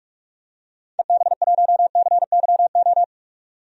Morse code, heard as a keyed tone, about 44 wpm, 690 Hz. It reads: EB1CYO
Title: DXCC-100 Call Signs - 6 Characters - 45wpm